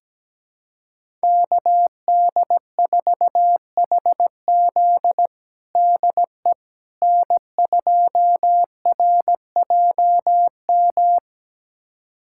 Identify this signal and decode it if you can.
Morse code, heard as a keyed tone, about 17 wpm, 700 Hz. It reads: KD4HZ DE N2RJM
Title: USA Callsign DE Callsign - 5 Characters - 17wpm